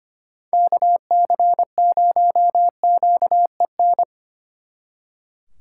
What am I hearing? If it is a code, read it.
Morse code, heard as a keyed tone, about 31 wpm, 700 Hz. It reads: KC0QEN
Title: Speed Racing - USA Call Signs 25wpm